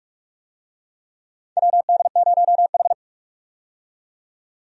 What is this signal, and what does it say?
Morse code, heard as a keyed tone, about 46 wpm, 700 Hz. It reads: WD0H
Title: USA Call Signs - 4 Characters - 45wpm